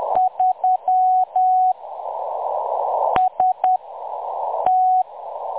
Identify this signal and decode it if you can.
Morse code, heard as a keyed tone, about 10 wpm, spaced as for 6 wpm, 740 Hz. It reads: 3 ST
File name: M23-ST3- -0858utc-10184khz